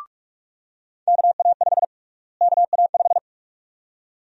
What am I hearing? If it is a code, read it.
Morse code, heard as a keyed tone, about 46 wpm, 700 Hz. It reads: KA5 KA5
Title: USA Call Sign Prefixes - Repeated 2x - 45wpm